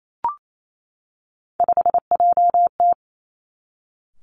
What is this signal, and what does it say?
Morse code, heard as a keyed tone, about 28 wpm, 700 Hz. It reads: 5JT
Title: Three Challenging Letter-Number Combinations 28wpm